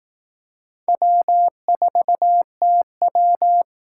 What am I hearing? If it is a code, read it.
Morse code, heard as a keyed tone, about 18 wpm, 700 Hz. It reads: W4TW
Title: Speed Racing - USA 1x2 Call Signs 15wpm